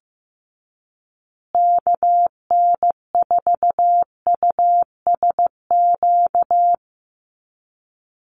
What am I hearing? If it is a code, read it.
Morse code, heard as a keyed tone, about 15 wpm, 700 Hz. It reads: KN4USQ
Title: USA Call Signs Repeated 3x 15wpm